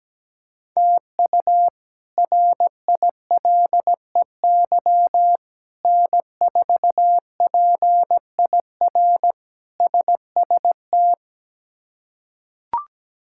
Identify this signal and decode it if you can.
Morse code, heard as a keyed tone, about 17 wpm, 700 Hz. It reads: TU RILEY N4PIR SST
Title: K1USN SST QSO Practice - 17wpm